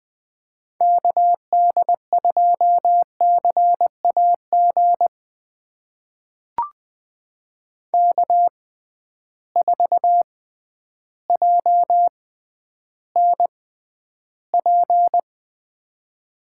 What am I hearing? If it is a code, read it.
Morse code, heard as a keyed tone, about 20 wpm, spaced as for 11 wpm, 700 Hz. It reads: KD2CAG K4JNP
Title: USA Call Signs - 6x Character Spacing - 20wpm